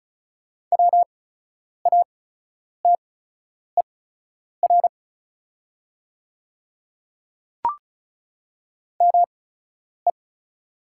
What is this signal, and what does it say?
Morse code, heard as a keyed tone, about 34 wpm, spaced as for 10 wpm, 690 Hz. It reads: WATER ME
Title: 8x Character Spacing - Top 200 Words - 35wpm